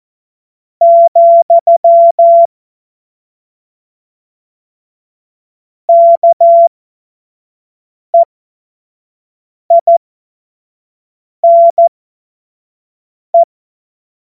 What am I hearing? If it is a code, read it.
Morse code, heard as a keyed tone, about 14 wpm, spaced as for 5 wpm, 680 Hz. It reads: , KEINE
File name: Morsekurs / Witze / Witz 29. Juni 2021, 25 F70